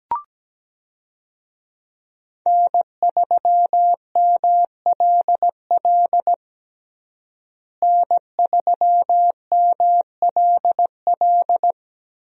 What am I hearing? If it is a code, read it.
Morse code, heard as a keyed tone, about 17 wpm, 700 Hz. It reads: N3MLL N3MLL
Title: USA Call Signs - 5 Characters - Repeated 3x - 17wpm